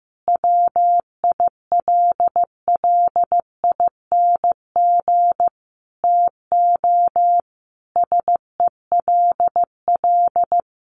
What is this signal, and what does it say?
Morse code, heard as a keyed tone, about 15 wpm, 700 Hz. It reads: WILLING TO SELL
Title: Sentences from Top 1100 Words 15wpm